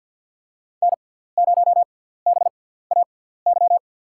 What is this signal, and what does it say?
Morse code, heard as a keyed tone, about 53 wpm, spaced as for 17 wpm, 700 Hz. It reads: N0BAY
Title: USA Call Signs - 5 Characters - 6x Character Spacing - 50wpm